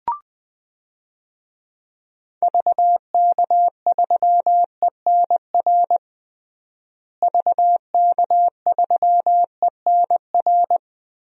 Morse code, heard as a keyed tone, about 20 wpm, 700 Hz. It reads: VK3ENR VK3ENR
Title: DXCC-100 Call Signs - 6 Characters - Repeated 3x - 20wpm